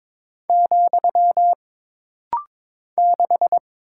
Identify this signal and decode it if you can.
Morse code, heard as a keyed tone, about 22 wpm, 700 Hz. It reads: , 6
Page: This rapid-fire timing is geared to push your ICR (Instant Character Recognition) skills.